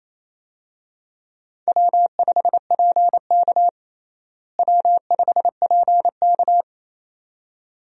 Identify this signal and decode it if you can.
Morse code, heard as a keyed tone, about 28 wpm, 700 Hz. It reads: W5PK W5PK
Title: USA Call Signs - 4 Characters - Repeated 2x - 28wpm